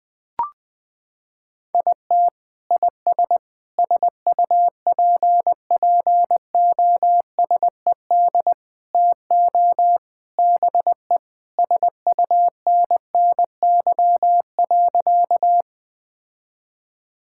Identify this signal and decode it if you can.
Morse code, heard as a keyed tone, about 20 wpm, 700 Hz. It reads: IT IS SUPPOSED TO BE SUNNY.
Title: Sentences with 3-Word Phrases - 20wpm